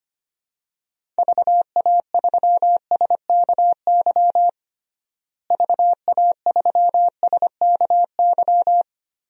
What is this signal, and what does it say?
Morse code, heard as a keyed tone, about 25 wpm, 700 Hz. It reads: VA3SKY VA3SKY
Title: DXCC-100 Call Signs - 6 Characters - Repeated 3x - 25wpm